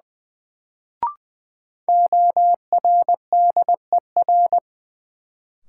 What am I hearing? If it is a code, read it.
Morse code, heard as a keyed tone, about 20 wpm, 700 Hz. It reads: ORDER
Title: Sets of 2 Words - New Word Drill - 20wpm